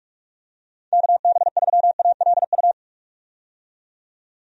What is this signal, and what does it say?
Morse code, heard as a keyed tone, about 44 wpm, 700 Hz. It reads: KB3ALU
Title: USA Call Signs - 6 Characters - 45wpm